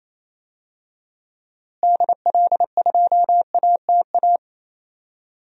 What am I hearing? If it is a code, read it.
Morse code, heard as a keyed tone, about 28 wpm, 700 Hz. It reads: DL2ATA
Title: DXCC-100 Call Signs - 6 Characters - Repeated 2x - 28wpm